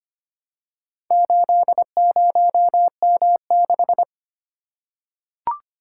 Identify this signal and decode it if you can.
Morse code, heard as a keyed tone, about 25 wpm, 700 Hz. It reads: 80M6
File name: practice / 25wpm / Four Letter-Number Combinations - 25wpm